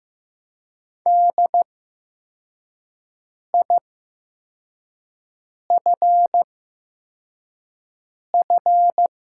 Morse code, heard as a keyed tone, about 15 wpm, spaced as for 4 wpm, 700 Hz. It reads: DIFF
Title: 8x Character Spacing - Top 200 Words - 15wpm